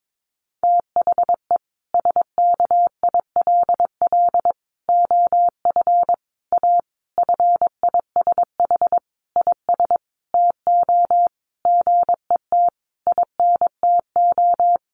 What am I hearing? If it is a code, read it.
Morse code, heard as a keyed tone, about 22 wpm, 700 Hz. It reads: THE SKILL OF A FISH IS TO GET INTO
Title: Sentences from Top 1400 Words 22wpm